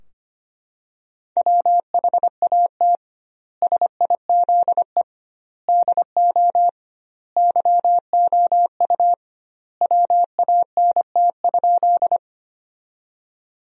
Morse code, heard as a keyed tone, about 25 wpm, 700 Hz. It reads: WHAT SIZE DO YOU WANT?
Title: 2x Word-Spacing - Sentences from Top 400 Words - 25wpm